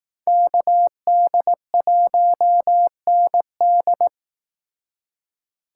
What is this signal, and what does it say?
Morse code, heard as a keyed tone, about 18 wpm, 700 Hz. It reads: KD1ND
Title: Speed Racing - USA 2x2 Call Signs 15wpm